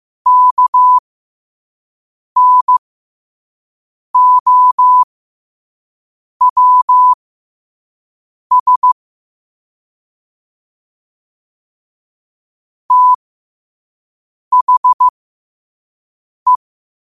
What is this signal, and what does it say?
Morse code, heard as a keyed tone, about 15 wpm, spaced as for 5 wpm, 1000 Hz. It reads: KNOWS THE
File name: qotdfiles / 5wpm